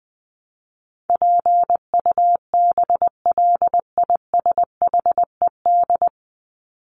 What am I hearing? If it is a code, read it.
Morse code, heard as a keyed tone, about 20 wpm, 700 Hz. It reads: PUBLISHED
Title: ED Suffix - Top 100 Words with Base Words 20wpm